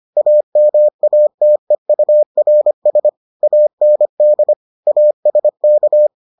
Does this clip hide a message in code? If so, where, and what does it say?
Yes: Morse code from 0.2 to 6.1 s, heard as a keyed tone, about 25 wpm, 600 Hz. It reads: AMATEURS AND ASK